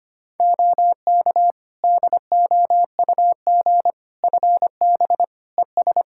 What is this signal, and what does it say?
Morse code, heard as a keyed tone, about 25 wpm, 700 Hz. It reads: OK DOUG FB ES
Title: Practice QSO - Exchange 2 - Info - 25wpm